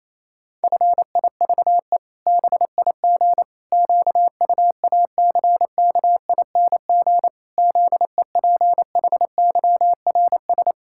Morse code, heard as a keyed tone, about 28 wpm, 700 Hz. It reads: FIVE BIG QUACKING ZEPHYRS
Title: Pangram Challenge v2 - 28wpm